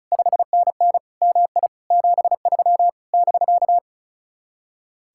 Morse code, heard as a keyed tone, about 42 wpm, 700 Hz. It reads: 5NN MI 73 <BK>
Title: Speed-Racing Common QSO Phrases - 35wpm